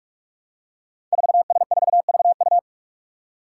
Morse code, heard as a keyed tone, about 44 wpm, 700 Hz. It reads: 4S4VU
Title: Five Challenging Letter-Number Combinations 45wpm